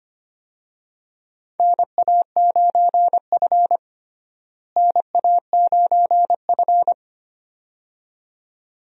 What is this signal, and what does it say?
Morse code, heard as a keyed tone, about 25 wpm, 700 Hz. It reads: NA9F NA9F
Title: USA Call Signs - 4 Characters - Repeated 2x - 25wpm